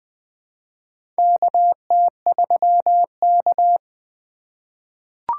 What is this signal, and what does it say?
Morse code, heard as a keyed tone, about 20 wpm, 700 Hz. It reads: KT3K
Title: USA Call Signs - 4 Characters - 3x Character Spacing - 20wpm